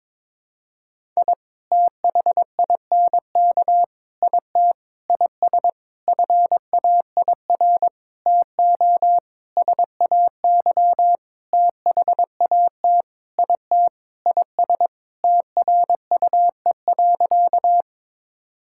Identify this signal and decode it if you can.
Morse code, heard as a keyed tone, about 22 wpm, 700 Hz. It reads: I THINK IT IS FAIR TO SAY THAT IT IS TRUE.
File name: practice / 22wpm / Sentences with 8-Word Phrases - 22wpm